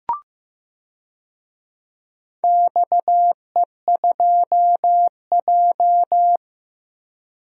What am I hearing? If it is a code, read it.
Morse code, heard as a keyed tone, about 16 wpm, 700 Hz. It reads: XE2J
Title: DXCC-100 Call Signs - 4 Characters - 15wpm